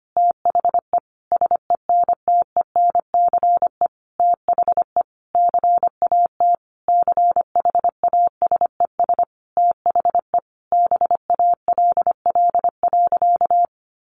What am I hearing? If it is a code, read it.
Morse code, heard as a keyed tone, about 25 wpm, 700 Hz. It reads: THE SENTENCE THE CAT CHASES THE BALL.
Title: Additional Easy Phrases 25wpm